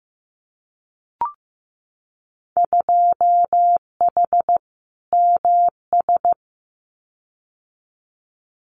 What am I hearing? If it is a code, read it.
Morse code, heard as a keyed tone, about 15 wpm, 700 Hz. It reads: 2H MS
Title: Winter Field Day Exchanges 15wpm